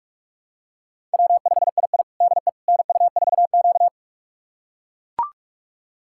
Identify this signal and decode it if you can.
Morse code, heard as a keyed tone, about 41 wpm, 700 Hz. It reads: W5II DE NU4Q
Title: USA Callsign DE Callsign - 4 Characters - 45wpm